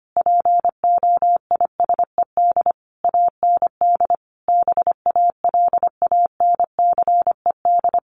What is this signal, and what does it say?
Morse code, heard as a keyed tone, about 25 wpm, 700 Hz. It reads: POISED AND BALANCED
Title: A Sound Of Thunder 25wpm